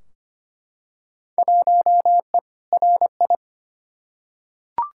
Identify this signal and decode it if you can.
Morse code, heard as a keyed tone, about 25 wpm, 700 Hz. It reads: 1E RI
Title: Field Day Exchanges Encore - 25wpm